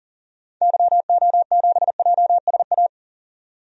Morse code, heard as a keyed tone, about 40 wpm, 700 Hz. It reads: YO7JSA
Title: DXCC-100 Call Signs - 6 Characters - 40wpm